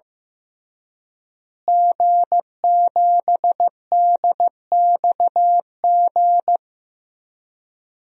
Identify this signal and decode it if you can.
Morse code, heard as a keyed tone, about 15 wpm, 700 Hz. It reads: G7DXG
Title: DXCC-100 Call Signs - 5 Characters - Repeated 3x - 15wpm